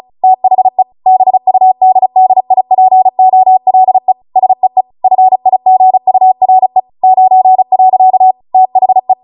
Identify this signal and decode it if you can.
Morse code, heard as a keyed tone, about 36 wpm, 750 Hz. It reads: THE BUDDIPOLE SEE FIGURE 9. THE